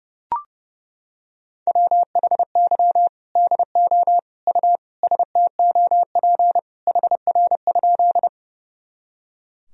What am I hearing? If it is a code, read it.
Morse code, heard as a keyed tone, about 32 wpm, 700 Hz. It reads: WHY DO U STOP HR?
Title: Sentences from Top 300 Words and Abbreviations 30wpm